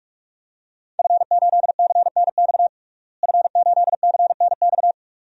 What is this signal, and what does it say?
Morse code, heard as a keyed tone, about 46 wpm, 700 Hz. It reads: F8CNX F8CNX
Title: DXCC-100 Call Signs - 5 Characters - Repeated 2x - 45wpm